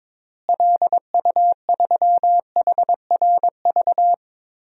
Morse code, heard as a keyed tone, about 21 wpm, 700 Hz. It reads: LU3HRV
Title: DXCC-100 Call Signs - 6 Characters - Repeated 3x - 22wpm